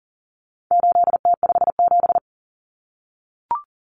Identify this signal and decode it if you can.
Morse code, heard as a keyed tone, about 40 wpm, 700 Hz. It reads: 8T57
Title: Common Cut Numbers - 4-Digit 40wpm